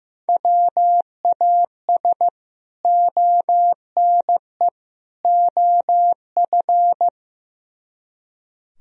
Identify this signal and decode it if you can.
Morse code, heard as a keyed tone, about 16 wpm, 700 Hz. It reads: WAS ONE OF
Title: Speed-Racing - Sets of 3 Words 15wpm